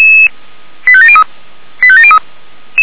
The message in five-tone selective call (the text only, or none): 85703 85703